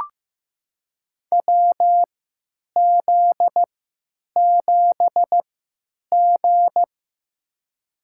Morse code, heard as a keyed tone, about 15 wpm, spaced as for 9 wpm, 700 Hz. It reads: WZ7G
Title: USA Call Signs - 4 Characters - 3x Character Spacing - 15wpm